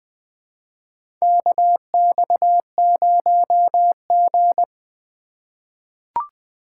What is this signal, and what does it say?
Morse code, heard as a keyed tone, about 20 wpm, 700 Hz. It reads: KX0G
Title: USA Call Signs - 4 Characters - 3x Character Spacing - 20wpm